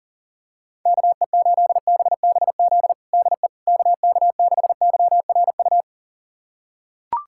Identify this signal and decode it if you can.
Morse code, heard as a keyed tone, about 40 wpm, 700 Hz. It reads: KE8BBZ DE KK6YRU
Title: USA Callsign DE Callsign - 6 Characters - 40wpm